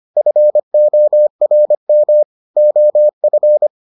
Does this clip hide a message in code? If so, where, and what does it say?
Yes: Morse code from 0.2 to 3.7 s, heard as a keyed tone, about 25 wpm, 600 Hz. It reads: FORM OF